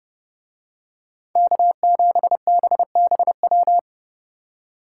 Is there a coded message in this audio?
Yes: Morse code from 1.4 to 3.8 s, heard as a keyed tone, about 29 wpm, 700 Hz. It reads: K7BBW